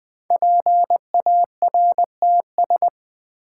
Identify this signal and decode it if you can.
Morse code, heard as a keyed tone, about 20 wpm, 700 Hz. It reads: PARTS
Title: Sets of 3 Words - New Word Drill - 20wpm